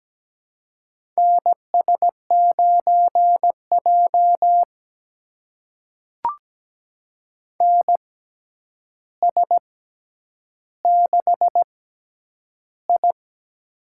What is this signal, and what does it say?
Morse code, heard as a keyed tone, about 17 wpm, spaced as for 9 wpm, 700 Hz. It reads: NS9J NS6I
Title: USA Call Signs - 4 Characters - 6x Character Spacing - 17wpm